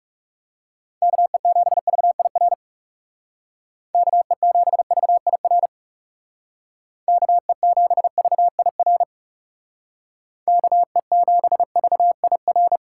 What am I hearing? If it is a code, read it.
Morse code, heard as a keyed tone, about 37 wpm, 700 Hz. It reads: KE7VIR KE7VIR KE7VIR KE7VIR
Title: Speed Racing - USA 2x3 Call Signs 30wpm